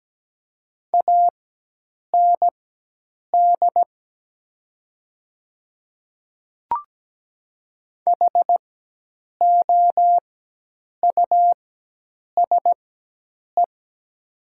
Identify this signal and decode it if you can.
Morse code, heard as a keyed tone, about 17 wpm, spaced as for 8 wpm, 700 Hz. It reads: AND HOUSE
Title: 4x Character Spacing - Top 300 Words - 17wpm